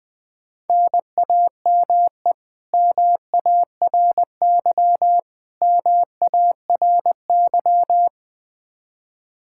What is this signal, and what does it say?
Morse code, heard as a keyed tone, about 20 wpm, 700 Hz. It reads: NAME MARY MARY
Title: K1USN SST QSO Practice 20wpm